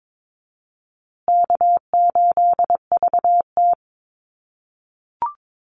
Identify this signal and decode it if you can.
Morse code, heard as a keyed tone, about 22 wpm, 700 Hz. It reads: K8VT
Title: USA Call Signs Repeated 2x 22wpm